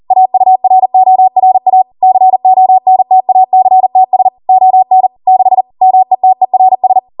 Morse code, heard as a keyed tone, about 40 wpm, 750 Hz. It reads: AURORA CONTACTS ON 6 METERS